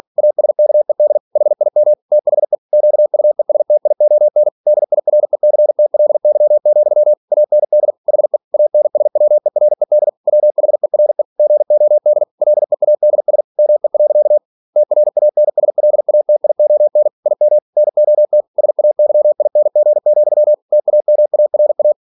Morse code, heard as a keyed tone, about 46 wpm, 600 Hz. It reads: ASKED HIM THE QUESTION DIRECTLY, AND HE ANSWERED WHERE GOD LEADS ME. TRANSLATION IM NOT SAYING, TAMARA
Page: Never 47wpm